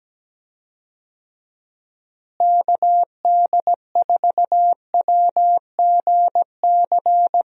Morse code, heard as a keyed tone, about 17 wpm, 700 Hz. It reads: KD4WGC
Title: USA Call Signs - 6 Characters - Repeated 3x - 17wpm